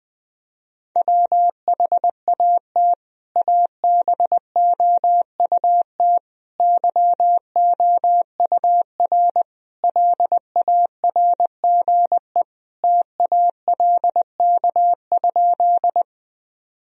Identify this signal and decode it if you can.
Morse code, heard as a keyed tone, about 21 wpm, 700 Hz. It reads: WHAT ABOUT YOUR LARGE TALK?
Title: Sentences from Top 1000 Words - Review Part 1 - 20wpm